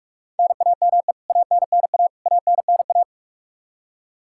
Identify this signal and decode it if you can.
Morse code, heard as a keyed tone, about 44 wpm, 700 Hz. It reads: NAME ANNA ANNA
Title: Practice QSO - Exchange 1 - Beginning - 45wpm